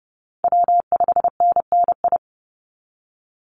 Morse code, heard as a keyed tone, about 30 wpm, 700 Hz. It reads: W5NNI
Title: USA Call Signs 30wpm